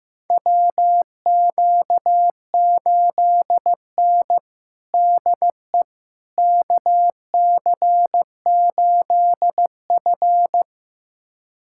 Morse code, heard as a keyed tone, about 15 wpm, 700 Hz. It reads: WQ8N DE KC8F
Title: USA Callsign DE Callsign - 4 Characters - 15wpm